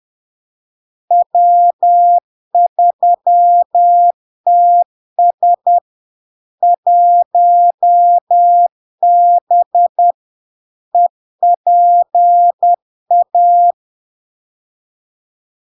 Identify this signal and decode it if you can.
Morse code, heard as a keyed tone, about 10 wpm, 700 Hz. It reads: W3TS 1B EPA
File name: LARK-FD-2022-Log-in-CW-10wpm-700-Hz